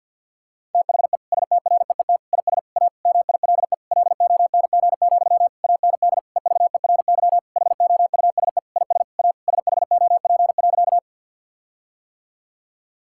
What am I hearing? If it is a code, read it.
Morse code, heard as a keyed tone, about 51 wpm, 700 Hz. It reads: THE STREET IS A MILE LONG, AND EVERY HOUSE IS A SHOP.
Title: Sentences from Top 1600 Words 50wpm